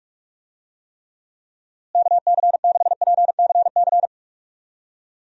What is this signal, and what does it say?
Morse code, heard as a keyed tone, about 46 wpm, 700 Hz. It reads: KC6PCC
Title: USA Call Signs - 6 Characters - Repeated 3x - 45wpm